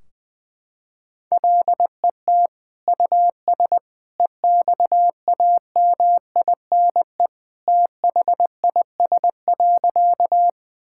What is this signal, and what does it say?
Morse code, heard as a keyed tone, about 21 wpm, 700 Hz. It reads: LET US EXAMINE THIS.
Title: Sentences from Top 1600 Words 20wpm